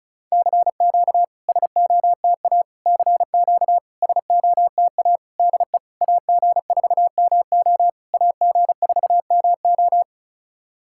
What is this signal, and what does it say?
Morse code, heard as a keyed tone, about 34 wpm, 700 Hz. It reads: CQ SOTA CQ SOTA DE AG4MO AG4MO
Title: SOTA QSO Practice 35wpm